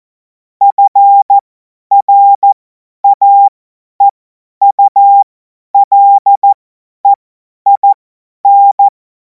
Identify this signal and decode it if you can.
Morse code, heard as a keyed tone, about 14 wpm, spaced as for 10 wpm, 800 Hz. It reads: FRAEULEIN
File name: Morsekurs / Witze / Witz 14. Juli 2026 50 F70